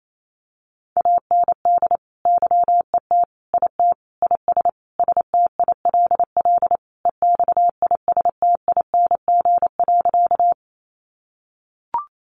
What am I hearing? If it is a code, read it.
Morse code, heard as a keyed tone, about 28 wpm, 700 Hz. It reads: AND YET IT IS STILL EXISTING.
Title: Sentences from Top 1500 Words 28wpm